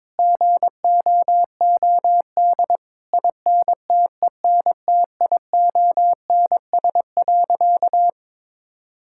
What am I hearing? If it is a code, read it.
Morse code, heard as a keyed tone, about 22 wpm, 700 Hz. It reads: GOOD INTENTIONS.
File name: practice / 22wpm / Sentences with 8-Word Phrases - 22wpm